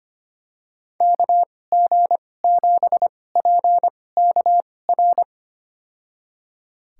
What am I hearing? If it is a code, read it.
Morse code, heard as a keyed tone, about 25 wpm, spaced as for 18 wpm, 700 Hz. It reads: KG7PKR
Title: USA Call Signs - 6 Characters - 2x Character Spacing - 25wpm